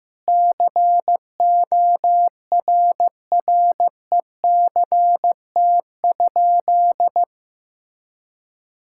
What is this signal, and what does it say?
Morse code, heard as a keyed tone, about 15 wpm, 700 Hz. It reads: CORRECT?
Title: Sentences from Top 1000 Words - Review Part 6 - 15wpm